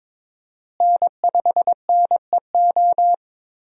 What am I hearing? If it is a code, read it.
Morse code, heard as a keyed tone, about 22 wpm, 700 Hz. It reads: N5NEO
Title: USA Call Signs - 5 Characters - Repeated 2x - 22wpm